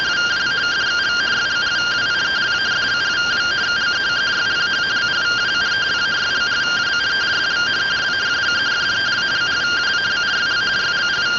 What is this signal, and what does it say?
Two-tone teleprinter signal, about 170 Hz shift.